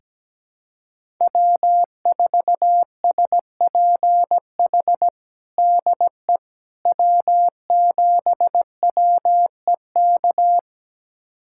Morse code, heard as a keyed tone, about 17 wpm, 700 Hz. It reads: W4SPH DE W7WEK
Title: USA Callsign DE Callsign - 5 Characters - 17wpm